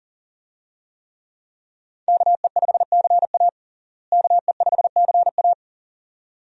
Morse code, heard as a keyed tone, about 40 wpm, 700 Hz. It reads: KE5CA KE5CA
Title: USA Call Signs - 5 Characters - Repeated 2x - 40wpm